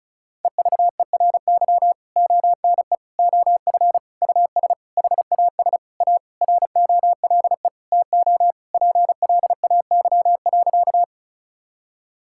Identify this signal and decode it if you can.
Morse code, heard as a keyed tone, about 36 wpm, 700 Hz. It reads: EVERY ONE OF US HAS A ROLE TO PLAY.
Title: Sentences from Top 1000 Words - Review Part 6 - 35wpm